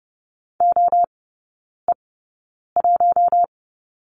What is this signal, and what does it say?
Morse code, heard as a keyed tone, about 30 wpm, spaced as for 9 wpm, 690 Hz. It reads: OE1
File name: Lesson 17e - Letter u - Three Characters 30wpm